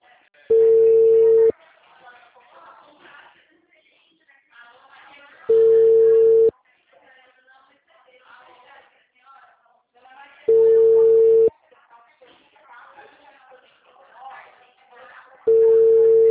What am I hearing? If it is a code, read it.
Ringback tone (China, 450 Hz), with speech.